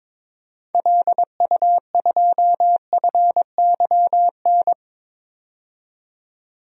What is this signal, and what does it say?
Morse code, heard as a keyed tone, about 22 wpm, 700 Hz. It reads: LU2FYN
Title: DXCC-100 Call Signs - 6 Characters - 22wpm